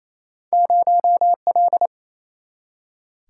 Morse code, heard as a keyed tone, about 28 wpm, 700 Hz. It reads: 0L
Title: Single-Letter Single-Number 28wpm